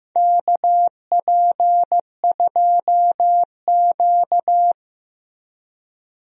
Morse code, heard as a keyed tone, about 15 wpm, 700 Hz. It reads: KP2Q
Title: DXCC-100 Call Signs - 4 Characters - Repeated 2x - 15wpm